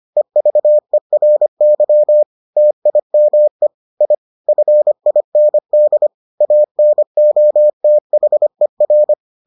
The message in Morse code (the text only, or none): EVERY TIME I FIND ANOTHER